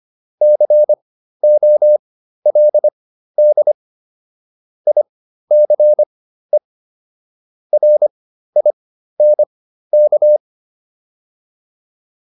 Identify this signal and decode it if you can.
Morse code, heard as a keyed tone, about 25 wpm, spaced as for 13 wpm, 600 Hz. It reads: COLD ICE RINK